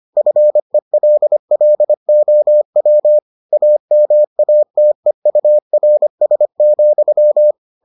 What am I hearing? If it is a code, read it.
Morse code, heard as a keyed tone, about 25 wpm, 600 Hz. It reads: FELLOW AMATEURS,